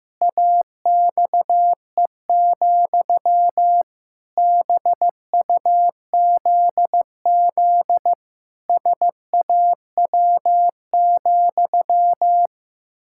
Morse code, heard as a keyed tone, about 15 wpm, 700 Hz. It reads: AXE, BUZZ SAW,
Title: Pangram Challenge v2 - 15wpm